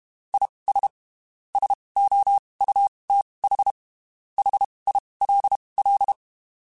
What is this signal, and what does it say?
Morse code, heard as a keyed tone, about 32 wpm, spaced as for 23 wpm, 800 Hz. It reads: IS SOUTH HILL
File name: (spaced) / QSOs Spaced-mp3s / kc7sac @30 sp